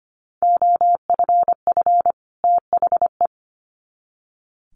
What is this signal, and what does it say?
Morse code, heard as a keyed tone, about 25 wpm, 700 Hz. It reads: OFF THE
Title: Sets of 2 Words 25wpm